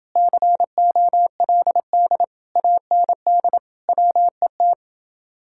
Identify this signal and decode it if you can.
Morse code, heard as a keyed tone, about 30 wpm, 700 Hz. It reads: COLD AND WET